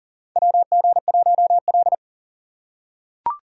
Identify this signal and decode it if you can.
Morse code, heard as a keyed tone, about 38 wpm, 700 Hz. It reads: WG1L
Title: USA Call Signs - 2x Character Spacing - 40wpm